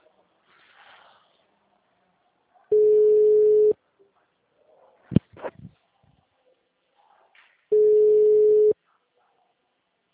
Ringback tone (Europe, 425 Hz).